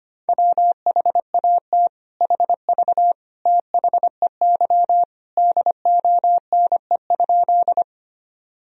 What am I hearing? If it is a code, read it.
Morse code, heard as a keyed tone, about 25 wpm, 700 Hz. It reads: WHAT HV THEY DONE?
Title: Sentences from Top 500 Words and Abbreviations 25wpm